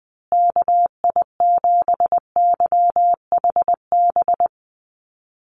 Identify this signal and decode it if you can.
Morse code, heard as a keyed tone, about 20 wpm, 700 Hz. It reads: KI7YHB
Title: USA Call Signs Repeated 3x 20wpm